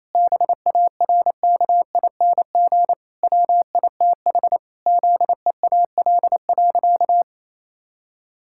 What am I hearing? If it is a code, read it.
Morse code, heard as a keyed tone, about 28 wpm, 700 Hz. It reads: BARKING WITH ZEAL.
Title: Pangram Challenge v2 - 28wpm